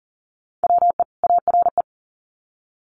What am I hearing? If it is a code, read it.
Morse code, heard as a keyed tone, about 40 wpm, 700 Hz. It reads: WE ARE
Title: Sets of 2 Words 40wpm